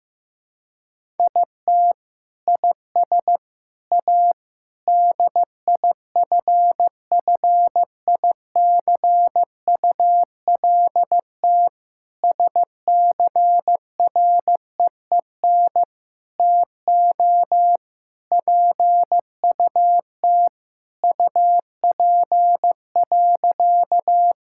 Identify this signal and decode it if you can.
Morse code, heard as a keyed tone, about 15 wpm, 700 Hz. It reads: IT IS A DIFFICULT SCREEN TO PUT UP.
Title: Sentences from Top 1000 Words - Review Part 6 - 15wpm